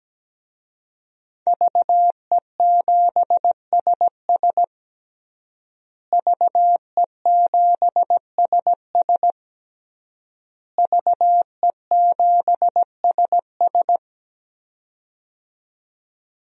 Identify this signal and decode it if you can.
Morse code, heard as a keyed tone, about 17 wpm, 700 Hz. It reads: VE7SS VE7SS VE7SS
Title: DXCC-100 Call Signs - 5 Characters - Repeated 3x - 17wpm